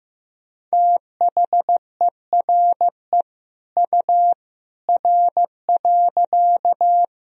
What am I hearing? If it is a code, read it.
Morse code, heard as a keyed tone, about 15 wpm, 700 Hz. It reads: THERE U R.
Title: Sentences from Top 100 Words and Abbreviations 15wpm